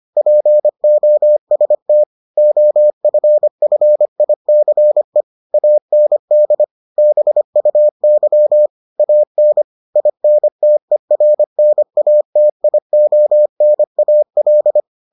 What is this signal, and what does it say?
Morse code, heard as a keyed tone, about 25 wpm, 600 Hz. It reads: POST OFFICE AND BUY AN INTERNATIONAL